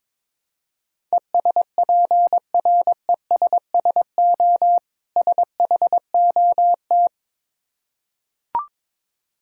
Morse code, heard as a keyed tone, about 22 wpm, 690 Hz. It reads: ESPRESSO SHOT
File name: Difficult Words with Hints - 22wpm